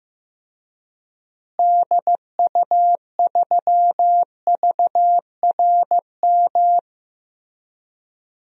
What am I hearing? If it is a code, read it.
Morse code, heard as a keyed tone, about 15 wpm, 700 Hz. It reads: DU3VRM
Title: DXCC-100 Call Signs - 6 Characters - Repeated 3x - 15wpm